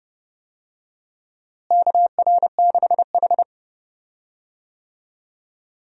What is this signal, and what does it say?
Morse code, heard as a keyed tone, about 31 wpm, 700 Hz. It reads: KR6H
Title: USA Call Signs - 4 Characters - Repeated 3x - 30wpm